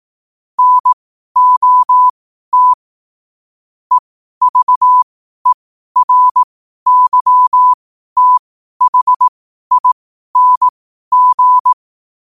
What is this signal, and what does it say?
Morse code, heard as a keyed tone, about 18 wpm, spaced as for 12 wpm, 1000 Hz. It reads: NOT EVERYTHING
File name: qotdfiles / 12wpm